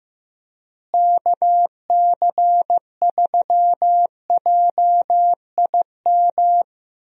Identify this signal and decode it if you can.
Morse code, heard as a keyed tone, about 14 wpm, 700 Hz. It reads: KC3JIM
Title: USA Call Signs - 6 Characters - Repeated 3x - 15wpm